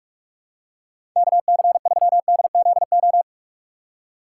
Morse code, heard as a keyed tone, about 45 wpm, 700 Hz. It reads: KC3DZO
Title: USA Call Signs - 3x Character Spacing - 45wpm